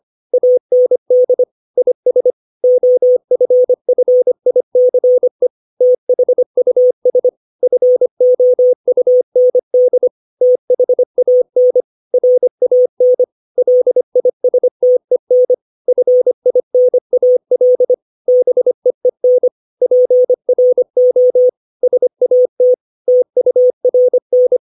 Morse code, heard as a keyed tone, about 25 wpm, 500 Hz. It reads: AND IS OFFICE THUS FOUND THAN RAN LISTEN FINAL BEEN PRO SAT TURN
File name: 632 words / 632 Words 25WPM